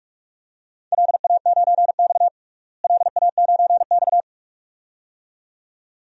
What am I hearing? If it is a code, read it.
Morse code, heard as a keyed tone, about 43 wpm, 700 Hz. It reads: LA9X LA9X
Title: DXCC-100 Call Signs - 4 Characters - Repeated 2x - 45wpm